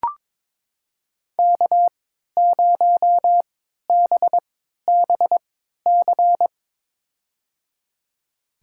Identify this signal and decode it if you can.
Morse code, heard as a keyed tone, about 22 wpm, spaced as for 13 wpm, 700 Hz. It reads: K0BBC
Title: USA Call Signs - 3x Character Spacing - 22wpm